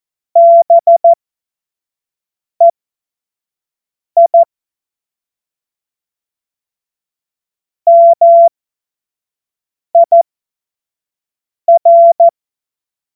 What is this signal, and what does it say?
Morse code, heard as a keyed tone, about 14 wpm, spaced as for 5 wpm, 680 Hz. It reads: BEI MIR
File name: Morsekurs / Witze / Witz 06. Juli 2021, 25 F70